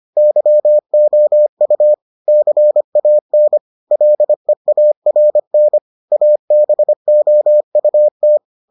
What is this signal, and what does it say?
Morse code, heard as a keyed tone, about 25 wpm, 600 Hz. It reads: YOU CAN LEARN ABOUT